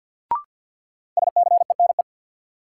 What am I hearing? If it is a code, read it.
Morse code, heard as a keyed tone, about 52 wpm, 700 Hz. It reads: SCENE
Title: Sentences from Top 1100 Words - New Word Drill - 50wpm